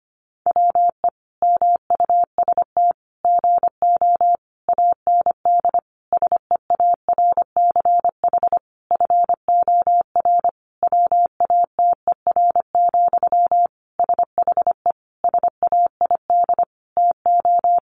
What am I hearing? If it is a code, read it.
Morse code, heard as a keyed tone, about 25 wpm, 700 Hz. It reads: WE MUST GO AND SEARCH FOR WATER, SHE SAID TO
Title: Sentences from Top 800 Words 25wpm